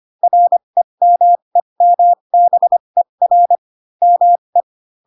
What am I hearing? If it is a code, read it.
Morse code, heard as a keyed tone, about 25 wpm, spaced as for 22 wpm, 700 Hz. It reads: REMEMBER ME
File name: Pinocchio / Pinocchio ch7 18 25 ik1hge